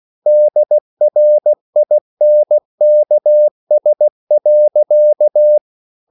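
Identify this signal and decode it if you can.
Morse code, heard as a keyed tone, about 16 wpm, 600 Hz. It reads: DRINKS.
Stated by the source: Slow (QRS) Morse code at 16 wpm, from CW med Gnister.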